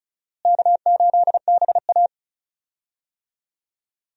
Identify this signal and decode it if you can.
Morse code, heard as a keyed tone, about 34 wpm, 700 Hz. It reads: K8BA
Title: USA Call Signs - 4 Characters - Repeated 3x - 35wpm